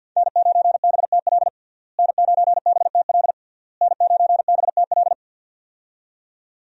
Morse code, heard as a keyed tone, about 52 wpm, 700 Hz. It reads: N9BTL N9BTL N9BTL
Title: USA Call Signs - 5 Characters - Repeated 3x - 50wpm